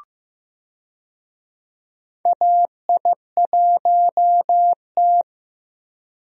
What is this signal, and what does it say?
Morse code, heard as a keyed tone, about 16 wpm, 700 Hz. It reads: AI1T
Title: USA Call Signs - 4 Characters - Repeated 2x - 15wpm